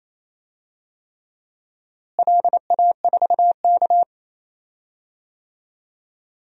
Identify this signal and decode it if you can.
Morse code, heard as a keyed tone, about 27 wpm, 700 Hz. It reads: LA4K
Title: DXCC-100 Call Signs - 4 Characters - Repeated 3x - 28wpm